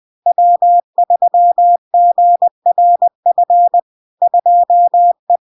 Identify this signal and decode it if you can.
Morse code, heard as a keyed tone, about 20 wpm, 700 Hz. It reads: W3GRF 2E
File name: LARK-FD-2022-Log-in-CW-20wpm-700-Hz